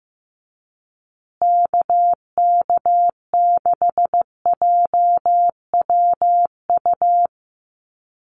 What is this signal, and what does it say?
Morse code, heard as a keyed tone, about 15 wpm, 700 Hz. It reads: KK6JWU
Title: USA Call Signs Repeated 3x 15wpm